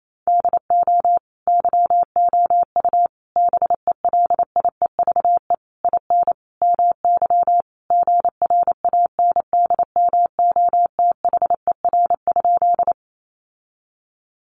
Morse code, heard as a keyed tone, about 28 wpm, 700 Hz. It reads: DO YOU BELIEVE IN MY GRANDMOTHER?
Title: Sentences from Top 1200 Words 28wpm